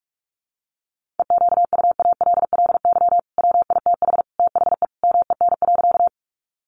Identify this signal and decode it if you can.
Morse code, heard as a keyed tone, about 44 wpm, 700 Hz. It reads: EQUALLY WITH THE MEN.
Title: Sentences from Top 1400 Words 45wpm